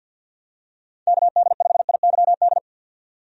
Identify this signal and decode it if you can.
Morse code, heard as a keyed tone, about 47 wpm, 700 Hz. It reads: KD5IYD
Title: USA Call Signs - 6 Characters - 50wpm